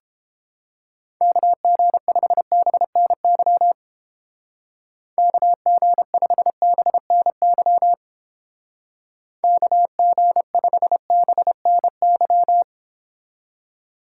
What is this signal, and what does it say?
Morse code, heard as a keyed tone, about 29 wpm, 700 Hz. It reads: KG5BNY KG5BNY KG5BNY